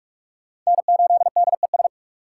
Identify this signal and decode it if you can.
Morse code, heard as a keyed tone, about 40 wpm, 700 Hz. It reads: N8DES